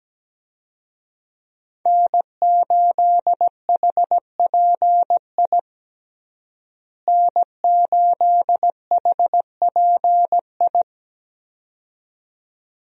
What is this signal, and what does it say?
Morse code, heard as a keyed tone, about 17 wpm, 700 Hz. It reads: N8HPI N8HPI
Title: USA Call Signs - 5 Characters - Repeated 2x - 17wpm